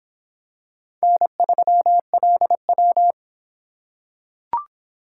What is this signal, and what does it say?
Morse code, heard as a keyed tone, about 23 wpm, 700 Hz. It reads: N3LW